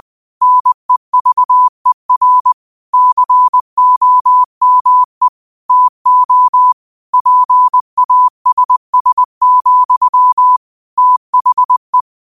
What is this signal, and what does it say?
Morse code, heard as a keyed tone, about 20 wpm, 1000 Hz. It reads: NEVER COME TO PASS, THE